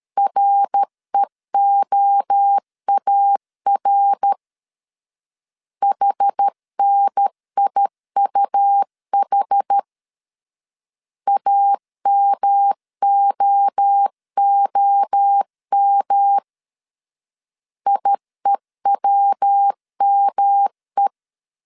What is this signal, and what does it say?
Morse code, heard as a keyed tone, about 13 wpm, 800 Hz. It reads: REOAR HNIUH AMOOM IEWME
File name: CW / DET / 1 3 1500C S A 12wpm 2905